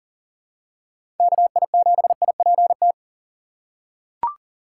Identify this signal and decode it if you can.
Morse code, heard as a keyed tone, about 38 wpm, 700 Hz. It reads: KI7IPT
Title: USA Call Signs - 6 Characters - 3x Character Spacing - 40wpm